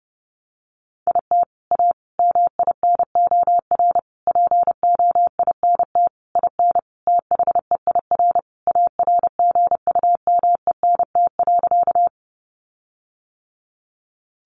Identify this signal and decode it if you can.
Morse code, heard as a keyed tone, about 30 wpm, 700 Hz. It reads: IT A MINOR POINT IN THEIR ARGUMENT.
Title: Sentences from Top 1500 Words 30wpm